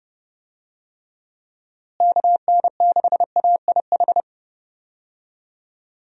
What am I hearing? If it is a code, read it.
Morse code, heard as a keyed tone, about 30 wpm, 700 Hz. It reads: KN6AIH
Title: USA Call Signs - 6 Characters - Repeated 2x - 30wpm